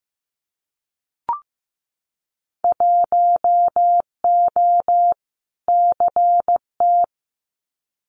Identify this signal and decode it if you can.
Morse code, heard as a keyed tone, about 15 wpm, 700 Hz. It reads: 1O CT
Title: Winter Field Day Exchanges 15wpm